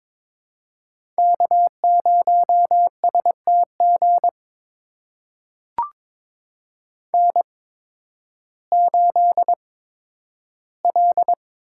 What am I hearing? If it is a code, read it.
Morse code, heard as a keyed tone, about 22 wpm, spaced as for 12 wpm, 700 Hz. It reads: K0STG N8L
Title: USA Call Signs - 8x Character Spacing - 22wpm